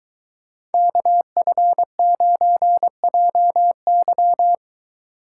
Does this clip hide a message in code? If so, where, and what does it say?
Yes: Morse code from 0.7 to 4.5 s, heard as a keyed tone, about 22 wpm, 700 Hz. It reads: KF9JY